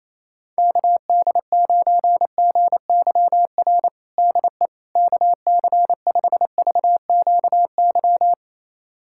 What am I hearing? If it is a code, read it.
Morse code, heard as a keyed tone, about 27 wpm, 700 Hz. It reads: KD9GYR DE KC5VQY
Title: USA Callsign DE Callsign - 6 Characters - 28wpm